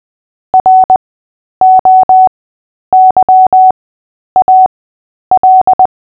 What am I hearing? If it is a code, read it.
Morse code, heard as a keyed tone, about 20 wpm, spaced as for 10 wpm, 750 Hz. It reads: ROYAL
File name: Short-Story-7-10wpm